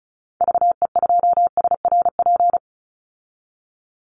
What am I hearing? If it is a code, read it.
Morse code, heard as a keyed tone, about 34 wpm, 700 Hz. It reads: VE2SRP
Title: Call Signs Repeated 3x - 35wpm